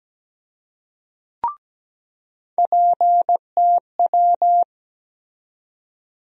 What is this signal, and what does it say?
Morse code, heard as a keyed tone, about 17 wpm, 700 Hz. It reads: PTW
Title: USA Call Sign Postfixes - 3 Letters 17wpm